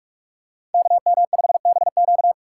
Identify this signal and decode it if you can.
Morse code, heard as a keyed tone, about 46 wpm, 700 Hz. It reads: KM5BQ
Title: USA Call Signs - 5 Characters - 6x Character Spacing - 45wpm